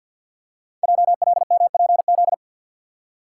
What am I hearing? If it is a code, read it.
Morse code, heard as a keyed tone, about 51 wpm, 700 Hz. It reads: JLMPZ
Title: Code Groups - 5 Letters - 50wpm